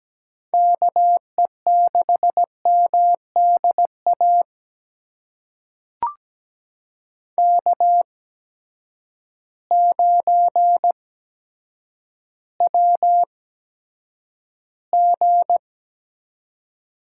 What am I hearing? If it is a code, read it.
Morse code, heard as a keyed tone, about 17 wpm, spaced as for 8 wpm, 700 Hz. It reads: KE6MDA K9WG
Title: USA Call Signs - 8x Character Spacing - 17wpm